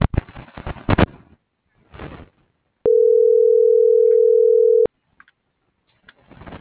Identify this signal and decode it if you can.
Ringback tone (North America), 440 + 480 Hz.